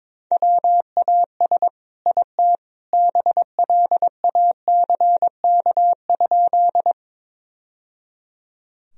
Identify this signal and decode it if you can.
Morse code, heard as a keyed tone, about 22 wpm, 700 Hz. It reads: WAS IT BLACK?
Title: Sentences from Top 1000 Words - Review Part 2 - 22wpm